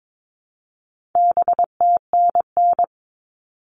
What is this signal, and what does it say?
Morse code, heard as a keyed tone, about 22 wpm, 700 Hz. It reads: BTNN
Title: All Cut Numbers - 4-Digit 22wpm